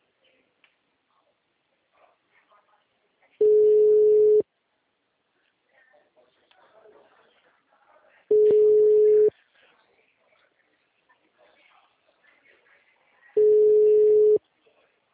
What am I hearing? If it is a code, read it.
Ringback tone (Europe, 425 Hz), with speech.